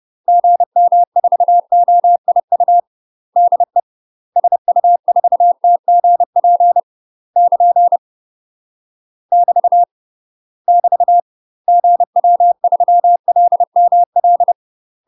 Morse code, heard as a keyed tone, about 30 wpm, 700 Hz. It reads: GM4OIU DE SU4TGP <KN> <BT> <BT> GW3LML
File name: UK MASTER QSOs TRAINING / 30WPM / UK MASTER QSOs 0050